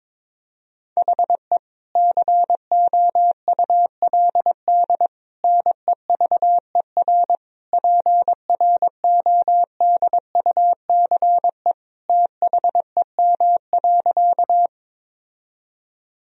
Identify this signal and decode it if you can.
Morse code, heard as a keyed tone, about 22 wpm, 700 Hz. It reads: HE COULD NEVER PRODUCE THEM.
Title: Sentences from Top 1000 Words - Review Part 3 - 22wpm